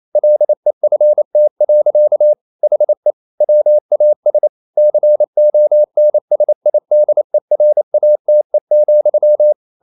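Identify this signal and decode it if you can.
Morse code, heard as a keyed tone, about 28 wpm, 600 Hz. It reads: LEFT. HE WAS CONSIDERATE,
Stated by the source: Morse code practice at 28 wpm, text of Ken Follett's novel Never in English, from CW med Gnister.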